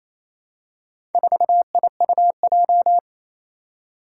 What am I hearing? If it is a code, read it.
Morse code, heard as a keyed tone, about 27 wpm, 700 Hz. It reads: 4IUJ
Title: Four Challenging Letter-Number Combinations 28wpm